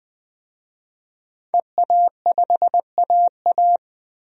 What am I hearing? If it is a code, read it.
Morse code, heard as a keyed tone, about 21 wpm, 700 Hz. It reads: EA5AA
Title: DXCC-100 Call Signs - 5 Characters - 20wpm